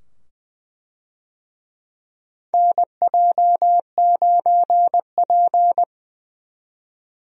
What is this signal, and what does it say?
Morse code, heard as a keyed tone, about 20 wpm, 700 Hz. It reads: NJ9P
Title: USA Call Signs - 4 Characters - 20wpm